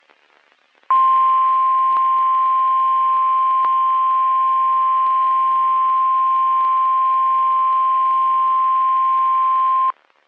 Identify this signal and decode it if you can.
Emergency Alert System (SAME) tones (the 1050 Hz attention tone). It is heard from 0.9 to 9.9 s.
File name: eas messages / EAS WXM31 Elmira